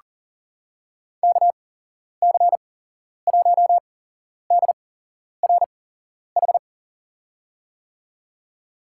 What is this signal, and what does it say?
Morse code, heard as a keyed tone, about 40 wpm, spaced as for 11 wpm, 700 Hz. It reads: KC1DRH
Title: USA Call Signs - 8x Character Spacing - 40wpm